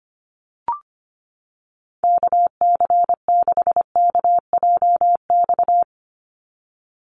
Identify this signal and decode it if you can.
Morse code, heard as a keyed tone, about 25 wpm, 690 Hz. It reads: KC6KJX
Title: USA Call Signs 25wpm